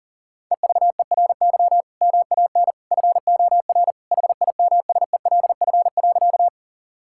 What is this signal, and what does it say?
Morse code, heard as a keyed tone, about 37 wpm, 700 Hz. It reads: EVERY MAN FOR HIMSELF.
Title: Sentences from Top 1600 Words 40wpm